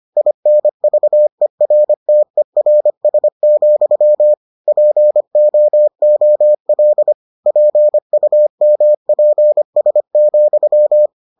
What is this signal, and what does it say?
Morse code, heard as a keyed tone, about 25 wpm, 600 Hz. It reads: INVERTERS, POOL PUMPS,